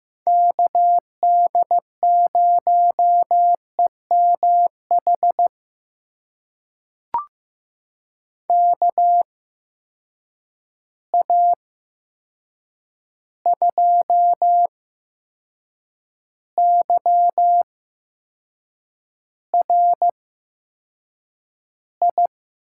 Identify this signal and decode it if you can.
Morse code, heard as a keyed tone, about 15 wpm, spaced as for 6 wpm, 700 Hz. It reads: KD0EMH KA2YRI
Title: USA Call Signs - 6 Characters - 8x Character Spacing - 15wpm